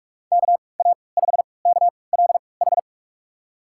Morse code, heard as a keyed tone, about 43 wpm, spaced as for 25 wpm, 690 Hz. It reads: KA5KLH
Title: USA Call Signs - 6 Characters - 3x Character Spacing - 45wpm